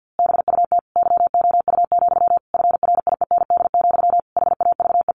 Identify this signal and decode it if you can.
Morse code, heard as a keyed tone, about 48 wpm, 700 Hz. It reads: BUT YOU, FRIEND, HAVE
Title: A Sound Of Thunder 50wpm